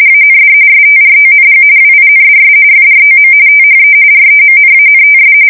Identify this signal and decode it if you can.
RTTY, about 45.45 baud. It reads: AND STRONG WINDS FROM THE REMNANTS
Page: 170 Shift 2125/2295